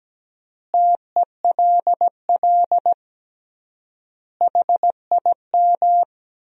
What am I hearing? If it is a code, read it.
Morse code, heard as a keyed tone, about 17 wpm, 700 Hz. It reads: TELL HIM
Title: 3x Word-Spacing - Sentences from Top 400 Words - 17wpm